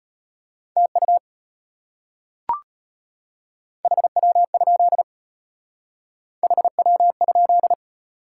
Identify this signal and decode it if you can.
Morse code, heard as a keyed tone, about 31 wpm, 700 Hz. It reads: TU HW? HW?